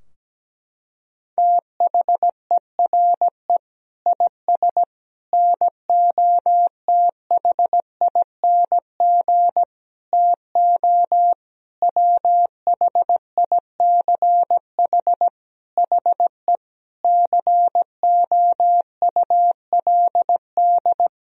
Morse code, heard as a keyed tone, about 17 wpm, 700 Hz. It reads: THERE IS NOTHING TO WHICH HE COULD
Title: Sentences from Top 1000 Words - Review Part 3 - 17wpm